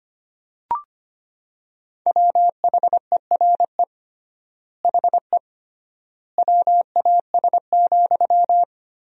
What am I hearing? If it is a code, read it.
Morse code, heard as a keyed tone, about 25 wpm, 700 Hz. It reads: WHERE HE WAS,
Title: 3x Word-Spacing - Sentences from Top 400 Words - 25wpm